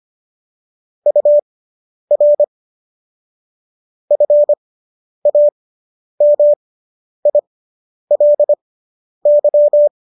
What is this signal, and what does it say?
Morse code, heard as a keyed tone, about 25 wpm, spaced as for 10 wpm, 590 Hz. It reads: UR FAMILY